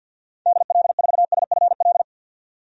Morse code, heard as a keyed tone, about 52 wpm, 700 Hz. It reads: DL4SFL
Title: DXCC-100 Call Signs - 6 Characters - 50wpm